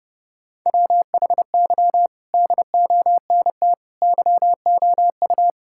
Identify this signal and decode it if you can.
Morse code, heard as a keyed tone, about 30 wpm, 700 Hz. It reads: WHY DONT YOU
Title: Sentences from Top 1000 Words - Review Part 5 - 30wpm